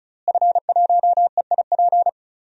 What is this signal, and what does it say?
Morse code, heard as a keyed tone, about 34 wpm, 700 Hz. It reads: F1EIP
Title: DXCC-100 Call Signs - 5 Characters - Repeated 3x - 35wpm